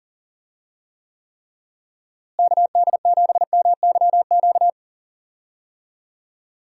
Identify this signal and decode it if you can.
Morse code, heard as a keyed tone, about 40 wpm, 700 Hz. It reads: KD7MYQ
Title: USA Call Signs - 6 Characters - Repeated 3x - 40wpm